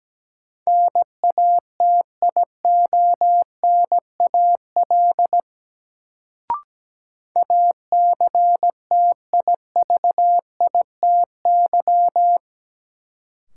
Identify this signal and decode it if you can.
Morse code, heard as a keyed tone, about 17 wpm, 700 Hz. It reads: NATIONAL ACTIVITY
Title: Sentences from Top 1000 Words - Review Part 4 - New Word Drill - 17wpm